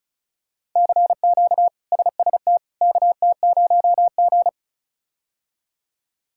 Morse code, heard as a keyed tone, about 36 wpm, 700 Hz. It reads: CQ SST KT0G
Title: K1USN SST QSO Practice - 35wpm